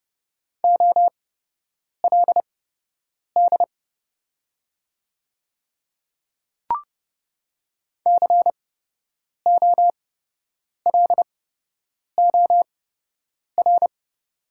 Morse code, heard as a keyed tone, about 29 wpm, spaced as for 8 wpm, 700 Hz. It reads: OLD COLOR
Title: 8x Character Spacing - Top 300 Words - 30wpm